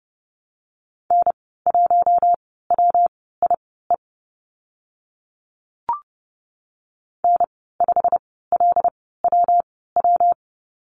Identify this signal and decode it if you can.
Morse code, heard as a keyed tone, about 30 wpm, spaced as for 17 wpm, 700 Hz. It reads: N1WIE N5LWW
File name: Lesson 19g - Letter w - Call Signs 30wpm